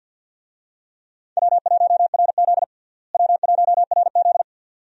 Morse code, heard as a keyed tone, about 53 wpm, 700 Hz. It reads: W1RZ W1RZ
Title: USA Call Signs - 4 Characters - Repeated 2x - 50wpm